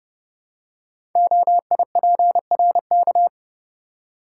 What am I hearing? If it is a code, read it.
Morse code, heard as a keyed tone, about 28 wpm, 700 Hz. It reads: OIPRK
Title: Code Groups - 5 Letters - 30wpm